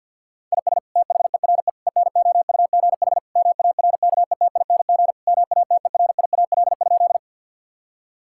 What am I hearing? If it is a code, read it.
Morse code, heard as a keyed tone, about 48 wpm, 690 Hz. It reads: IS THERE ENOUGH MARKETING MATERIAL?
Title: Sentences from Top 1000 Words - Review Part 5 - 50wpm